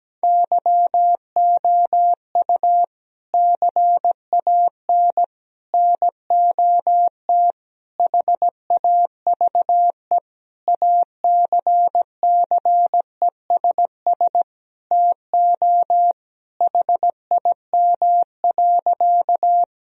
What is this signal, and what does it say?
Morse code, heard as a keyed tone, about 17 wpm, 700 Hz. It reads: YOU CAN NOT HAVE ACCESS TO HIM.
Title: Sentences from Top 1000 Words - Review Part 4 - 17wpm